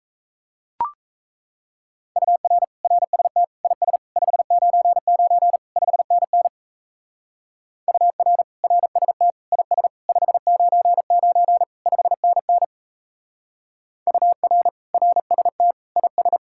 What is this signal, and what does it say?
Morse code, heard as a keyed tone, about 32 wpm, 700 Hz. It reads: UR RST IS 599 5NN UR RST IS 599 5NN UR RST IS
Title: Speed-Racing Common QSO Phrases Encore - 28wpm